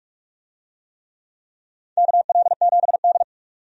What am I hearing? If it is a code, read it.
Morse code, heard as a keyed tone, about 46 wpm, 700 Hz. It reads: KL7D
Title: DXCC-100 Call Signs - 4 Characters - 45wpm